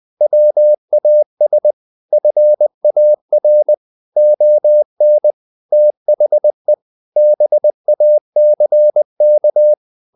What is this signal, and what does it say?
Morse code, heard as a keyed tone, about 20 wpm, 600 Hz. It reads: WAS FAR ON THE BACK
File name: the old man and the sea part 2 20wpm